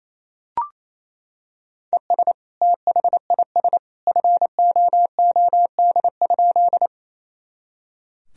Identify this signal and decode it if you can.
Morse code, heard as a keyed tone, about 28 wpm, 700 Hz. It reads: ES THIS FOOD?
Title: Sentences from Top 300 Words and Abbreviations 28wpm